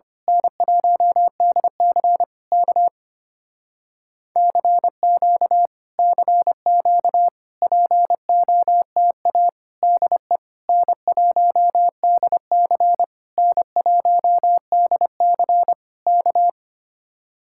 Morse code, heard as a keyed tone, about 24 wpm, 700 Hz. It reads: N1DC K CQ CQ POTA DE N1DC N1DC K
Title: Speed-Racing Common QSO Phrases - 25wpm